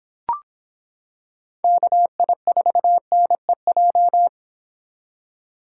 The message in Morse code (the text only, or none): KI4NEJ